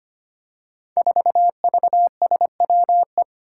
Morse code, heard as a keyed tone, about 25 wpm, 700 Hz. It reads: 4VSWE
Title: Five Challenging Letter-Number Combinations 25wpm